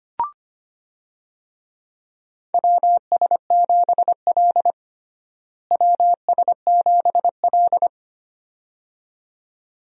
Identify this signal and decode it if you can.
Morse code, heard as a keyed tone, about 25 wpm, 700 Hz. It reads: WS7L WS7L
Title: USA Call Signs - 4 Characters - Repeated 2x - 25wpm